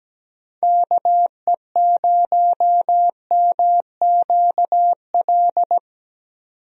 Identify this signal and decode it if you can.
Morse code, heard as a keyed tone, about 17 wpm, 700 Hz. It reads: KE0MQL
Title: USA Call Signs - 6 Characters - 17wpm